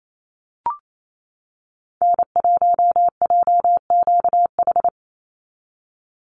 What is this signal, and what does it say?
Morse code, heard as a keyed tone, about 28 wpm, 700 Hz. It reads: N1JQH
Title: USA Call Signs 28wpm